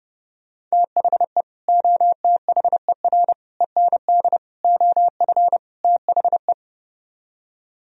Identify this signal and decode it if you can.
Morse code, heard as a keyed tone, about 38 wpm, 700 Hz. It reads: THE OTHER END OF THE